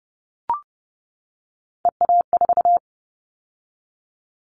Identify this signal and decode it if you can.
Morse code, heard as a keyed tone, about 30 wpm, 700 Hz. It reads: EA4
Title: Callsign Prefixes for Top 25 DXCC - 30wpm